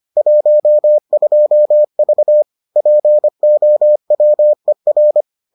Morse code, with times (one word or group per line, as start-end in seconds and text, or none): 0.2-2.4 12V
2.8-5.2 POWER